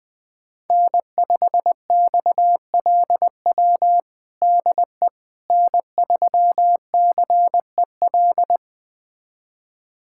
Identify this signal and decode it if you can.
Morse code, heard as a keyed tone, about 21 wpm, 700 Hz. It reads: N5XLW DE N3CEL
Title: USA Callsign DE Callsign 20wpm